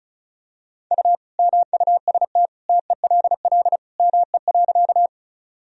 Morse code, heard as a keyed tone, about 34 wpm, 700 Hz. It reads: U MUST TELL ME.
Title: Sentences from Top 200 Words and Abbreviations 35wpm